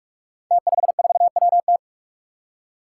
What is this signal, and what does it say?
Morse code, heard as a keyed tone, about 46 wpm, 700 Hz. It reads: T54WT
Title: Five Challenging Letter-Number Combinations 45wpm